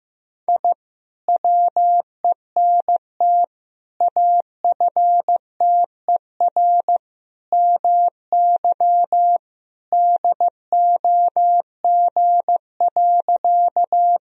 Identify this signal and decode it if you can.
Morse code, heard as a keyed tone, about 15 wpm, 700 Hz. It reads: I WENT AFTER MY DOG.
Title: Sentences from Top 1000 Words - Review Part 2 - 15wpm